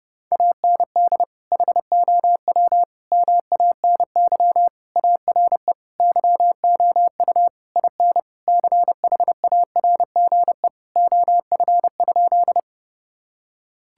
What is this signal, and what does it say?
Morse code, heard as a keyed tone, about 30 wpm, 700 Hz. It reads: AND HOW MANY ARE YOU IN CHARGE OF?
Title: Sentences from Top 1000 Words - Review Part 6 - 30wpm